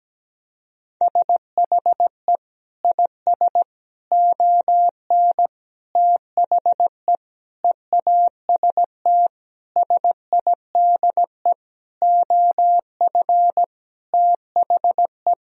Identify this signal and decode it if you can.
Morse code, heard as a keyed tone, about 17 wpm, 700 Hz. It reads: SHE IS ON THE EAST SIDE OF THE
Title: Sentences from Top 1000 Words - Review Part 3 - 17wpm